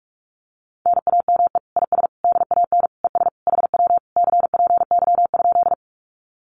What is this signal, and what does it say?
Morse code, heard as a keyed tone, about 46 wpm, 700 Hz. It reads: NAME IS DAN ES HW CPY?
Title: Common QSO Phrases Encore 45wpm